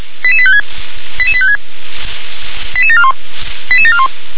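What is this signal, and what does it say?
Five-tone selective call (ZVEI), heard as quick runs of five tones, tones of about 70 ms. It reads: 88856 88856 88741 88741
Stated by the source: ZVEI five-tone call received by SDR.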